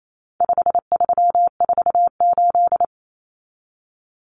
Morse code, heard as a keyed tone, about 28 wpm, 700 Hz. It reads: 5348
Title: Common Cut Numbers - 4-Digit 28wpm